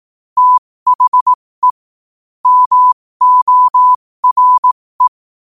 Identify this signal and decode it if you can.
Morse code, heard as a keyed tone, about 18 wpm, spaced as for 15 wpm, 1000 Hz. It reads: THE MORE
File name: qotdfiles / 15wpm